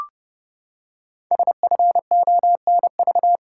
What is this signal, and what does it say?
Morse code, heard as a keyed tone, about 30 wpm, 700 Hz. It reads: SFONV
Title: Code Groups - 5 Letters - 30wpm